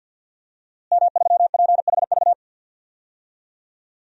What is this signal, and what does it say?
Morse code, heard as a keyed tone, about 51 wpm, 700 Hz. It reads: M3PHV
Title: DXCC-100 Call Signs - 5 Characters - Repeated 2x - 50wpm